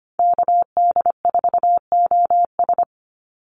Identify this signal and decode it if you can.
Morse code, heard as a keyed tone, about 25 wpm, 700 Hz. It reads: KD4OS
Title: Call Signs Repeated 3x - 25wpm